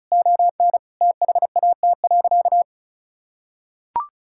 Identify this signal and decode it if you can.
Morse code, heard as a keyed tone, about 36 wpm, 700 Hz. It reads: ON THAT.
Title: Sentences from Top 1000 Words - Review Part 5 - 35wpm